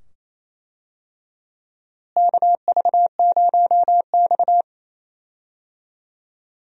Morse code, heard as a keyed tone, about 28 wpm, 700 Hz. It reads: KV0X
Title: USA Call Signs - 4 Characters - Repeated 2x - 28wpm